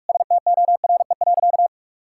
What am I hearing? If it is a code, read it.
Morse code, heard as a keyed tone, about 46 wpm, 690 Hz. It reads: STORE.
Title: Sentences from Top 1000 Words - Review Part 5 - 45wpm